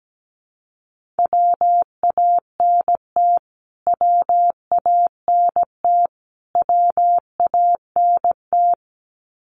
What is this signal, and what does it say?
Morse code, heard as a keyed tone, about 17 wpm, 700 Hz. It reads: WANT WANT WANT
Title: Top 200 Words Repeated 17wpm